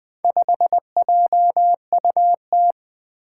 Morse code, heard as a keyed tone, about 20 wpm, 700 Hz. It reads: 5JUT
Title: Four Challenging Letter-Number Combinations 20wpm